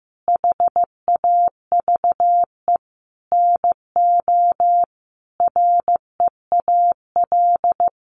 Morse code, heard as a keyed tone, about 15 wpm, 700 Hz. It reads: HAVE NO REAL
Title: Sentences from Top 1300 Words 15wpm